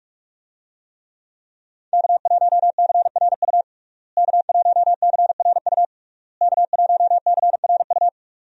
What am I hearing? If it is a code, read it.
Morse code, heard as a keyed tone, about 42 wpm, 700 Hz. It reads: K1CRU K1CRU K1CRU
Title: USA Call Signs - 5 Characters - Repeated 3x - 45wpm